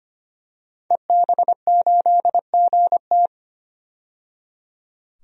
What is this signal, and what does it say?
Morse code, heard as a keyed tone, about 25 wpm, 700 Hz. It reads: EB8GT
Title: DXCC-100 Call Signs - 5 Characters - 25wpm